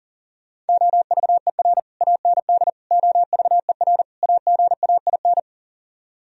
Morse code, heard as a keyed tone, about 46 wpm, 700 Hz. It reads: OVER AND OVER AGAIN
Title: Speed-Racing - Sets of 4 Words 40wpm